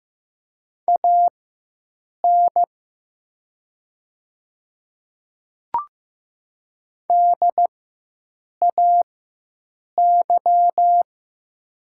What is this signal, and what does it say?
Morse code, heard as a keyed tone, about 15 wpm, spaced as for 7 wpm, 700 Hz. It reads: AN DAY
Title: 4x Character Spacing - Top 100 Words - 15wpm